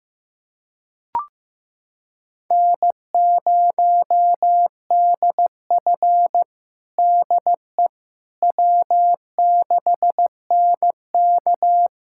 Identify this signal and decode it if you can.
Morse code, heard as a keyed tone, about 16 wpm, 700 Hz. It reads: N0DF DE W6NK
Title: USA Callsign DE Callsign - 4 Characters - 15wpm